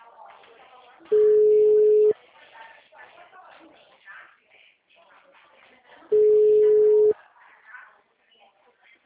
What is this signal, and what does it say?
Ringback tone (Europe, 425 Hz), with speech.